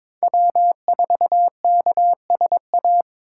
Morse code, heard as a keyed tone, about 26 wpm, 700 Hz. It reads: W4KSA
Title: Speed Racing - USA 1x3 Call Signs 22wpm